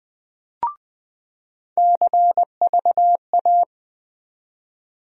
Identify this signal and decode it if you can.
Morse code, heard as a keyed tone, about 20 wpm, 700 Hz. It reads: CVA
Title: USA Call Sign Postfixes - 3 Letters 20wpm